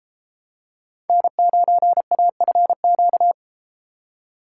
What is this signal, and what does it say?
Morse code, heard as a keyed tone, about 28 wpm, 700 Hz. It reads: N9AFQ